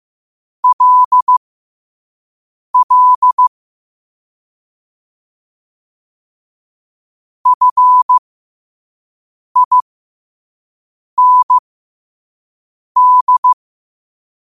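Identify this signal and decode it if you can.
Morse code, heard as a keyed tone, about 15 wpm, spaced as for 5 wpm, 1000 Hz. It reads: LL FIND
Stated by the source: Morse quote-of-the-day practice at 5 wpm.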